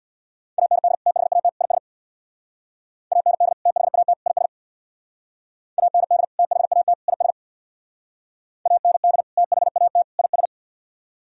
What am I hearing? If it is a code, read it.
Morse code, heard as a keyed tone, about 59 wpm, 700 Hz. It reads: AND THAT IS AND THAT IS AND THAT IS AND THAT IS
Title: Speed-Racing - Sets of 3 Words 50wpm